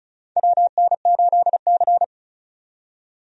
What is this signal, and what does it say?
Morse code, heard as a keyed tone, about 36 wpm, 700 Hz. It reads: WN8C
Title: USA Call Signs - 4 Characters - 35wpm